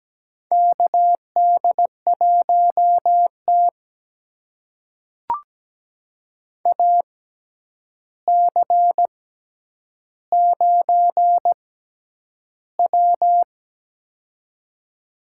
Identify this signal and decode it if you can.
Morse code, heard as a keyed tone, about 17 wpm, spaced as for 9 wpm, 700 Hz. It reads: KD1T AC9W
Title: USA Call Signs - 4 Characters - 6x Character Spacing - 17wpm